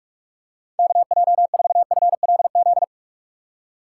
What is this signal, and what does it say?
Morse code, heard as a keyed tone, about 45 wpm, 700 Hz. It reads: KJ4FLZ
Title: USA Call Signs - 6x Character Spacing - 45wpm